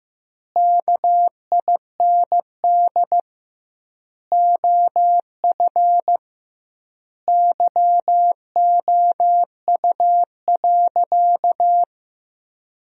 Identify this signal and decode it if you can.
Morse code, heard as a keyed tone, about 15 wpm, 700 Hz. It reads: KIND OF YOU.
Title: 2x Word-Spacing - Sentences from Top 200 Words - 15wpm